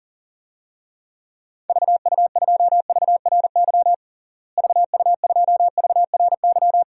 Morse code, heard as a keyed tone, about 40 wpm, 700 Hz. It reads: VU2VRY VU2VRY
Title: DXCC-100 Call Signs - 6 Characters - Repeated 2x - 40wpm